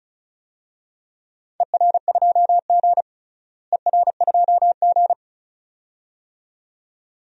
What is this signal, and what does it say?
Morse code, heard as a keyed tone, about 36 wpm, 690 Hz. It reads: ER2G ER2G
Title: DXCC-100 Call Signs - 4 Characters - Repeated 2x - 35wpm